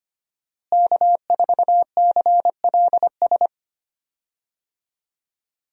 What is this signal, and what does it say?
Morse code, heard as a keyed tone, about 25 wpm, 700 Hz. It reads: K4CLS
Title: USA Call Signs - 5 Characters - Repeated 2x - 25wpm